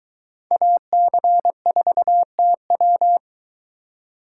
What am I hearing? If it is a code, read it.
Morse code, heard as a keyed tone, about 21 wpm, 700 Hz. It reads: AC4TW